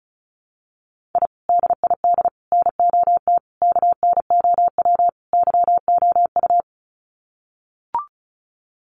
Morse code, heard as a keyed tone, about 34 wpm, 700 Hz. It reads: I DID NOT KNOW YOU
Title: Sets of 5 Words Encore2 35wpm